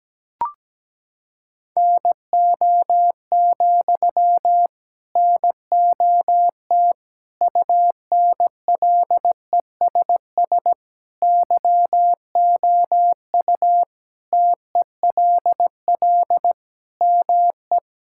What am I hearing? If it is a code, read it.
Morse code, heard as a keyed tone, about 17 wpm, 700 Hz. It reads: NO, NOT UNLESS YOU TELL ME
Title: Sentences from Top 1000 Words - Review Part 6 - 17wpm